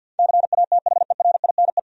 Morse code, heard as a keyed tone, about 52 wpm, 700 Hz. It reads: CATHERINE
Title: USA Common Names - 50wpm